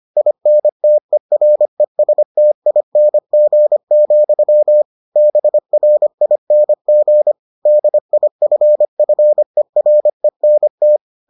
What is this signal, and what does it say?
Morse code, heard as a keyed tone, about 25 wpm, 600 Hz. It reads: INTERESTING, BRING DIFFERENT